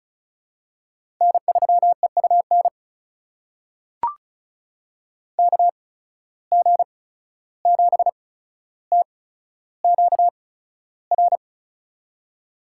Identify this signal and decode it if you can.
Morse code, heard as a keyed tone, about 36 wpm, spaced as for 14 wpm, 700 Hz. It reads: N3EUN KG7TQR
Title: USA Call Signs - 8x Character Spacing - 35wpm